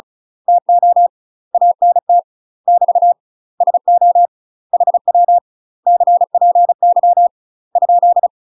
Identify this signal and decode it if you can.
Morse code, heard as a keyed tone, about 36 wpm, 700 Hz. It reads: TO ANT <BT> SO HW CPY ?
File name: UK MASTER QSOs TRAINING / 35WPM / UK MASTER QSOs 0017